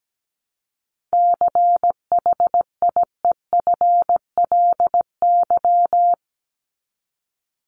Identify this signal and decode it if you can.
Morse code, heard as a keyed tone, about 17 wpm, 700 Hz. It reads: CHIEFLY
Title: LY Suffix - Top 100 Words with Base Words 17wpm